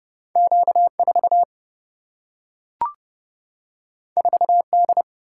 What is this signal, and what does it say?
Morse code, heard as a keyed tone, about 30 wpm, 700 Hz. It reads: Q 4 4 D
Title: Single-Letter Single-Number 30wpm